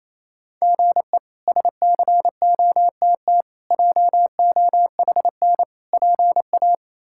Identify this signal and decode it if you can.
Morse code, heard as a keyed tone, about 28 wpm, 700 Hz. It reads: GE SCOTT JOHN PA
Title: K1USN SST QSO Practice - 28wpm